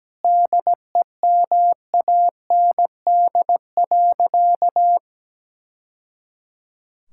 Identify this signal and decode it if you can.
Morse code, heard as a keyed tone, about 17 wpm, 700 Hz. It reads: DEMAND.
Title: Sentences from Top 1000 Words - Review Part 5 - 17wpm